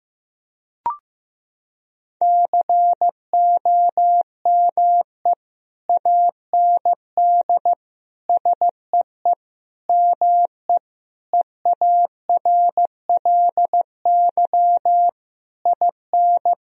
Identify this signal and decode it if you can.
Morse code, heard as a keyed tone, about 15 wpm, 700 Hz. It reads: COME AND SEE ME EARLY IN
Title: Sentences from Top 1000 Words - Review Part 2 - 15wpm